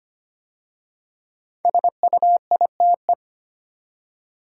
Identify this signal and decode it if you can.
Morse code, heard as a keyed tone, about 25 wpm, 700 Hz. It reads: SUITE
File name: 5 Letter Words - First 3 Letters Given - 25wpm